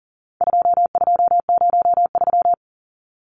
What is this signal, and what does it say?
Morse code, heard as a keyed tone, about 40 wpm, 700 Hz. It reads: 2203
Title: Numbers - 4-Digit 40wpm